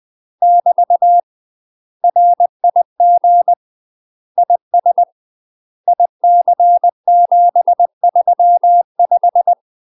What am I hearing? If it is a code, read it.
Morse code, heard as a keyed tone, about 20 wpm, 700 Hz. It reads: <BT> RIG IS IC735
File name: UK MASTER QSOs TRAINING / 20WPM / UK MASTER QSOs 0063